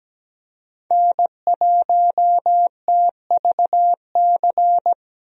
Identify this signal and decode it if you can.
Morse code, heard as a keyed tone, about 17 wpm, 700 Hz. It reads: N1TVC
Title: USA Call Signs - 5 Characters - Repeated 2x - 17wpm